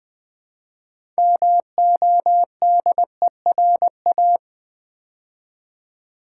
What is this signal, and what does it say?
Morse code, heard as a keyed tone, about 20 wpm, 700 Hz. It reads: MODERA
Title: Build A Word - 3 Letters At A Time - 8-Letter Words - 20wpm